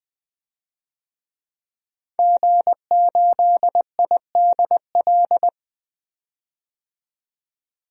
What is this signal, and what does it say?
Morse code, heard as a keyed tone, about 21 wpm, 700 Hz. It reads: G8IDL
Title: DXCC-100 Call Signs - 5 Characters - 20wpm